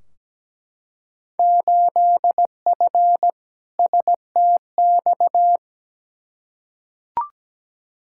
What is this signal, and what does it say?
Morse code, heard as a keyed tone, about 17 wpm, 700 Hz. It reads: 8F STX
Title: Field Day Exchanges Encore - 17wpm